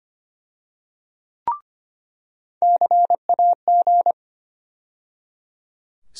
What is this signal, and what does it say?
Morse code, heard as a keyed tone, about 25 wpm, 700 Hz. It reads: CAG
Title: USA Call Sign Postfixes - 3 Letters 25wpm